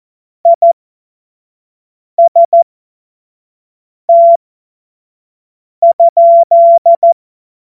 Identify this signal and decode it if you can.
Morse code, heard as a keyed tone, about 14 wpm, spaced as for 5 wpm, 680 Hz. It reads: IST?
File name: Morsekurs / Witze / Witz 24.Oktober 2023 25 F70